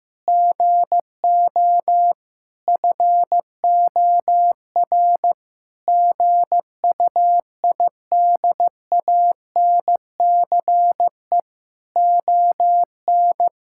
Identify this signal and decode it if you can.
Morse code, heard as a keyed tone, about 16 wpm, 700 Hz. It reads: GO FOR GUIDANCE ON
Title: Sentences from Top 1600 Words 15wpm